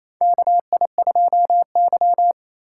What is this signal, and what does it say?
Morse code, heard as a keyed tone, about 27 wpm, 690 Hz. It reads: KI2Y
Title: USA Call Signs - 4 Characters - 2x Character Spacing - 28wpm